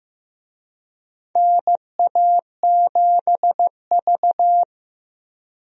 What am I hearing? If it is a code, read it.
Morse code, heard as a keyed tone, about 15 wpm, 700 Hz. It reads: NA7V
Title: USA Call Signs - 4 Characters - Repeated 2x - 15wpm